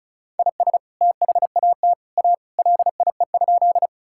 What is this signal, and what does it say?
Morse code, heard as a keyed tone, about 34 wpm, 700 Hz. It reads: IS THAT A LIE?
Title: Sentences from Top 1600 Words 35wpm